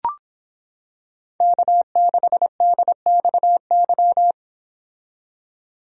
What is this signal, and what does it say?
Morse code, heard as a keyed tone, about 26 wpm, 700 Hz. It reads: K6DXY